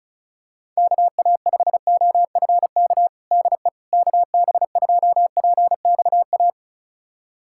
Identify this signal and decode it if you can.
Morse code, heard as a keyed tone, about 36 wpm, 700 Hz. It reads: KA5OFK DE KB2PXA
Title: USA Callsign DE Callsign - 6 Characters - 35wpm